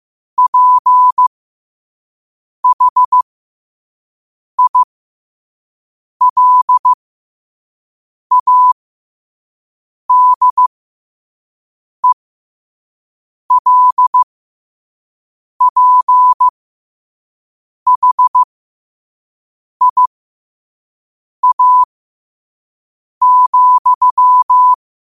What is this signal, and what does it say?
Morse code, heard as a keyed tone, about 15 wpm, spaced as for 5 wpm, 1000 Hz. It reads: PHILADELPHIA,
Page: New quotes every day in morse code at 5 Words per minute.